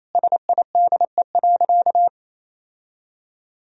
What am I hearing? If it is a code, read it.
Morse code, heard as a keyed tone, about 28 wpm, 700 Hz. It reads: SIDE.
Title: Sentences from Top 1000 Words - Review Part 1 - 28wpm